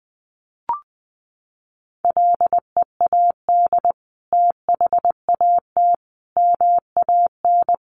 Morse code, heard as a keyed tone, about 20 wpm, 700 Hz. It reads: LEAD THAT MAN
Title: Sentences from Top 600 Words 20wpm